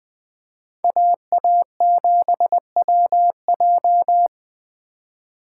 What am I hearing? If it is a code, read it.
Morse code, heard as a keyed tone, about 20 wpm, 700 Hz. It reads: AA7WJ